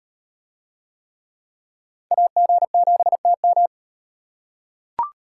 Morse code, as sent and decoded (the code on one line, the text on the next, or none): .- --. --... - --
AG7TM